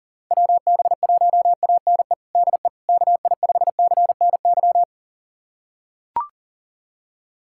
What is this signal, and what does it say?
Morse code, heard as a keyed tone, about 40 wpm, 700 Hz. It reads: WB1ANE DE KI5CNY
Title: USA Callsign DE Callsign - 6 Characters - 40wpm